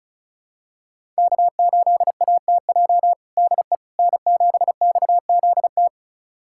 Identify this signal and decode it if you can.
Morse code, heard as a keyed tone, about 34 wpm, 700 Hz. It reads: K8ATJ DE N7XZT
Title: USA Callsign DE Callsign - 5 Characters - 35wpm